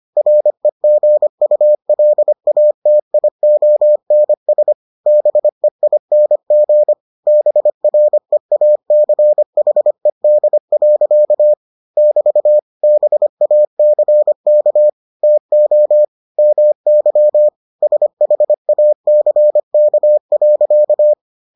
Morse code, heard as a keyed tone, about 25 wpm, 600 Hz. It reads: REGULATIONS BEING BREACHED. <BT> BACK TO MY SHACK.